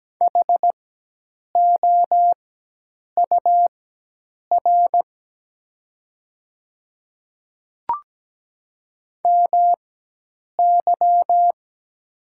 Morse code, heard as a keyed tone, about 17 wpm, spaced as for 8 wpm, 700 Hz. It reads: HOUR MY
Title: 4x Character Spacing - Top 500 Words - 17wpm